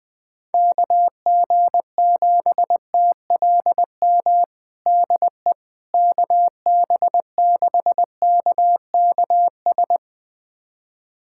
Morse code, heard as a keyed tone, about 20 wpm, 700 Hz. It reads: KG7TLM DE KB6KKS
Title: USA Callsign DE Callsign - 6 Characters - 20wpm